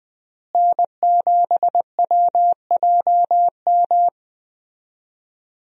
Morse code, heard as a keyed tone, about 21 wpm, 700 Hz. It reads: N7WJM
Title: USA Call Signs - 5 Characters - Repeated 3x - 20wpm